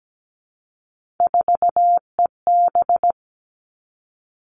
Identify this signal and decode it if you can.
Morse code, heard as a keyed tone, about 17 wpm, 700 Hz. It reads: 4EB
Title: All Cut Numbers - 3-Digit 17wpm